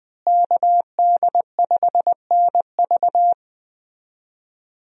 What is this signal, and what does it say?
Morse code, heard as a keyed tone, about 20 wpm, 700 Hz. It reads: KD5NV
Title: USA Call Signs - 5 Characters - Repeated 3x - 20wpm